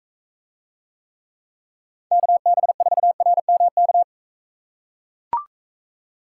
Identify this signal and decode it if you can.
Morse code, heard as a keyed tone, about 41 wpm, 700 Hz. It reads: KB4RMK